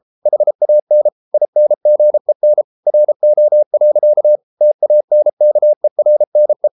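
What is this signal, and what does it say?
Morse code, heard as a keyed tone, about 33 wpm, 600 Hz. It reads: HAN INGEN RO. TANKERNE
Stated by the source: Morse code at 33 wpm, QRQ.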